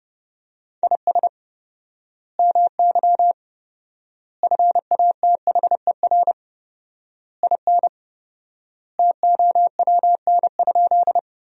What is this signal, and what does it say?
Morse code, heard as a keyed tone, about 32 wpm, 700 Hz. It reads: IS MY FATHER IN TOWN?
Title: 4x Word-Spacing - Sentences from Top 500 Words - 30wpm